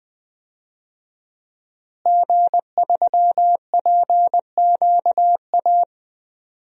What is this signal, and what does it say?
Morse code, heard as a keyed tone, about 20 wpm, 700 Hz. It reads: G3PQA
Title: DXCC-100 Call Signs - 5 Characters - Repeated 2x - 20wpm